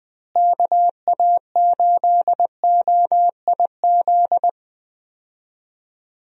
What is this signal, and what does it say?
Morse code, heard as a keyed tone, about 20 wpm, 700 Hz. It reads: KA8OIZ
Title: USA Call Signs - 6 Characters - Repeated 3x - 20wpm